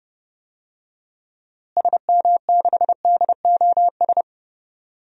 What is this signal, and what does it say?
Morse code, heard as a keyed tone, about 32 wpm, 700 Hz. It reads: SM6DOS
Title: DXCC-100 Call Signs - 6 Characters - Repeated 2x - 30wpm